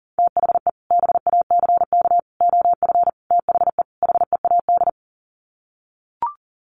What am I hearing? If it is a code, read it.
Morse code, heard as a keyed tone, about 40 wpm, 700 Hz. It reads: THE BACK OF THE HEAD
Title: Sets of 5 Words Encore2 40wpm